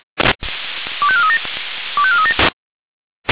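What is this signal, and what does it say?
Five-tone selective call (ZVEI), heard as quick runs of five tones, tones of about 70 ms. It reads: 25437 25437
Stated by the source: ZVEI five-tone selective call 25429 received via SDR.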